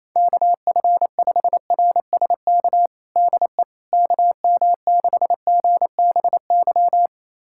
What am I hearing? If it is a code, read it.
Morse code, heard as a keyed tone, about 28 wpm, 700 Hz. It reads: KF5RSK DE KM6GBY
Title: USA Callsign DE Callsign - 6 Characters - 28wpm